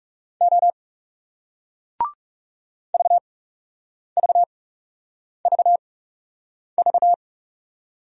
Morse code, heard as a keyed tone, about 40 wpm, spaced as for 8 wpm, 700 Hz. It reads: O VVVV